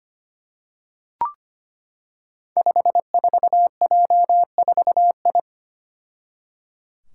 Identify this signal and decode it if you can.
Morse code, heard as a keyed tone, about 25 wpm, 700 Hz. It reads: 54J4I
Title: Five Challenging Letter-Number Combinations 25wpm